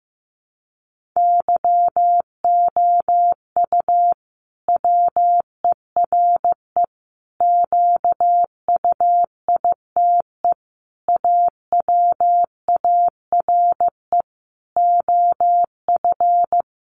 Morse code, heard as a keyed tone, about 15 wpm, 700 Hz. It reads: YOU WERE QUITE AWARE OF
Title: Sentences from Top 1200 Words 15wpm